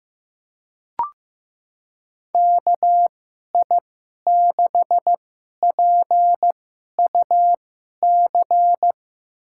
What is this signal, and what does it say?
Morse code, heard as a keyed tone, about 15 wpm, spaced as for 11 wpm, 700 Hz. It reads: KI6PUC
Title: USA Call Signs - 6 Characters - 2x Character Spacing - 15wpm